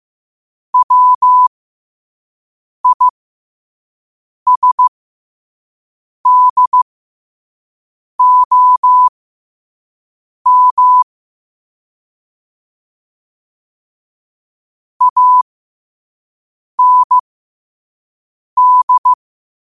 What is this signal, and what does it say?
Morse code, heard as a keyed tone, about 15 wpm, spaced as for 5 wpm, 1000 Hz. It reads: WISDOM AND
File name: qotdfiles / 5wpm